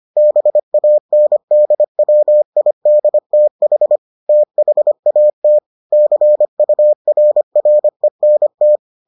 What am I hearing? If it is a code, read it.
Morse code, heard as a keyed tone, about 25 wpm, 600 Hz. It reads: BANDWIDTH THAT CURRENT